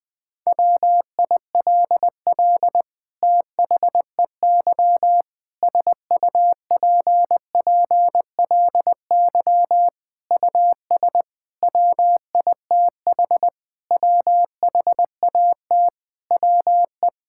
Morse code, heard as a keyed tone, about 21 wpm, 700 Hz. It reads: WILL THEY SUPPLY US WITH WHAT WE
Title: Sentences from Top 1000 Words - Review Part 5 - 20wpm